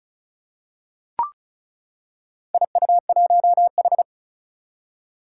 Morse code, heard as a keyed tone, about 36 wpm, 700 Hz. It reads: IU1H
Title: Four Challenging Letter-Number Combinations 35wpm